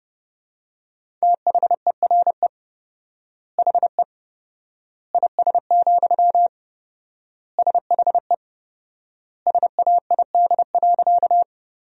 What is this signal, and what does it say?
Morse code, heard as a keyed tone, about 32 wpm, 700 Hz. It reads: THERE HE IS, SHE SAID.
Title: 4x Word-Spacing - Sentences from Top 100 Words - 30wpm